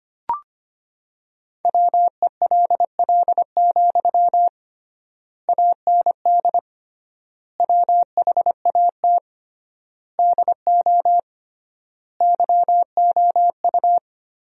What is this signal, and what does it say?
Morse code, heard as a keyed tone, about 25 wpm, 700 Hz. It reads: WELL, AND WHAT DO YOU
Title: 3x Word-Spacing - Sentences from Top 200 Words - 25wpm